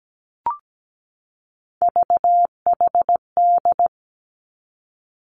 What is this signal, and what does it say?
Morse code, heard as a keyed tone, about 17 wpm, 700 Hz. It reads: VHD
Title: Three Challenging Letter-Number Combinations 17wpm